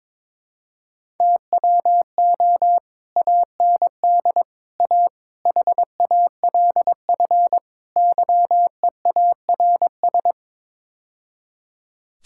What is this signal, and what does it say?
Morse code, heard as a keyed tone, about 22 wpm, 700 Hz. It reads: TWO AND A HALF YEARS
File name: practice / 22wpm / Build A Phrase - 5-Word Phrases - 22wpm